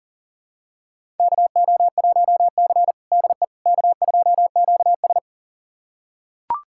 Morse code, heard as a keyed tone, about 37 wpm, 700 Hz. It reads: KO1C DE K2QS
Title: USA Callsign DE Callsign - 4 Characters - 40wpm